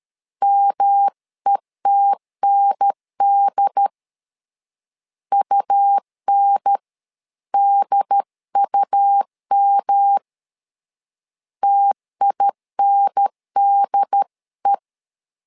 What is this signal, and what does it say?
Morse code, heard as a keyed tone, about 13 wpm, 800 Hz. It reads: METND UNDUM TINDE
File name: LEZIONI / 3 MWRO / 3 M 12wpm 1541